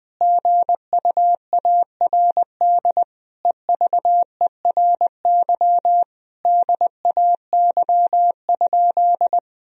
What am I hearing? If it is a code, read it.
Morse code, heard as a keyed tone, about 20 wpm, 700 Hz. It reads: GUARD EVERY DAY?
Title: Sentences from Top 1000 Words - Review Part 4 - 20wpm